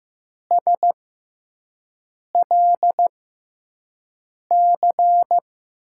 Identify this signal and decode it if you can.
Morse code, heard as a keyed tone, about 15 wpm, spaced as for 5 wpm, 700 Hz. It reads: SLC
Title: USA Call Signs - 5 Characters - 6x Character Spacing - 15wpm